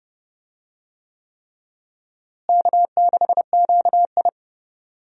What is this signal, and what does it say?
Morse code, heard as a keyed tone, about 30 wpm, 700 Hz. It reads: K6QI
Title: USA Call Signs - 4 Characters - Repeated 3x - 30wpm